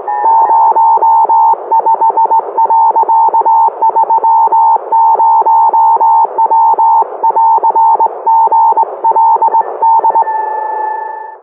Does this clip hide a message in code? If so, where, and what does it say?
Yes: Morse code from 0.2 to 10.2 s, heard as a keyed tone, about 18 wpm, 900 Hz. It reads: 05.30W <AR> GLD